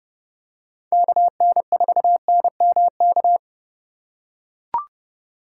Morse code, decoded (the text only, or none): KN4NMK